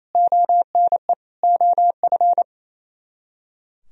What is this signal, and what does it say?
Morse code, heard as a keyed tone, about 35 wpm, 700 Hz. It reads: ONE OF
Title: Speed-Racing - Sets of 4 Words 28wpm